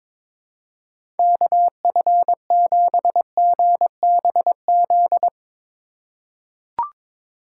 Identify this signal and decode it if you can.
Morse code, heard as a keyed tone, about 22 wpm, 700 Hz. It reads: KF7GBZ
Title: USA Call Signs - 6 Characters - 4x Character Spacing - 22wpm